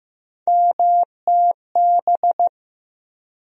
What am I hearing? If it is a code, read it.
Morse code, heard as a keyed tone, about 15 wpm, 700 Hz. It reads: MTB
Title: USA Call Sign Postfixes - 3 Letters 15wpm